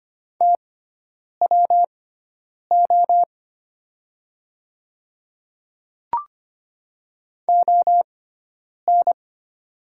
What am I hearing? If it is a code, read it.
Morse code, heard as a keyed tone, about 25 wpm, spaced as for 9 wpm, 700 Hz. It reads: TWO ON
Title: 6x Character Spacing - Top 100 Words - 25wpm